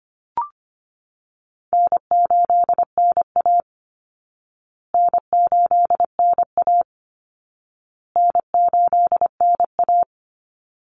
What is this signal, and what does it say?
Morse code, heard as a keyed tone, about 25 wpm, 700 Hz. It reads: N8NA N8NA N8NA
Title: Easy Call Signs Repeated 3x - 25wpm